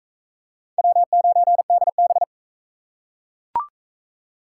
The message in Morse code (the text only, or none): W9DB